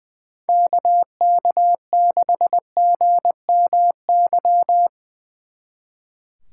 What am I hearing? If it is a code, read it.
Morse code, heard as a keyed tone, about 22 wpm, 700 Hz. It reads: KK6GMY
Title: Speed Racing - USA 2x3 Call Signs 20wpm